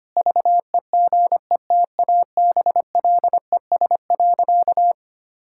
Morse code, heard as a keyed tone, about 25 wpm, 700 Hz. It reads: VEGETABLES.
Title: Pangram Challenge v2 - 25wpm